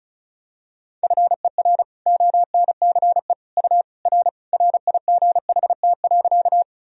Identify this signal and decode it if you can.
Morse code, heard as a keyed tone, about 36 wpm, 700 Hz. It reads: FER ONCE U R RIGHT.
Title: Sentences from Top 300 Words and Abbreviations 35wpm